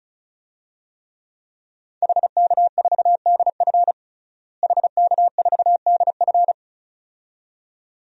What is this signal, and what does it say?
Morse code, heard as a keyed tone, about 36 wpm, 700 Hz. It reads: HK4DF HK4DF
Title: DXCC-100 Call Signs - 5 Characters - Repeated 2x - 35wpm